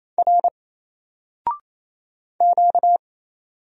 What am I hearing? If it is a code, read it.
Morse code, heard as a keyed tone, about 27 wpm, 700 Hz. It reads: R Q
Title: Single Character (Letter:Number:Punctuation:Prosign) Alt 28wpm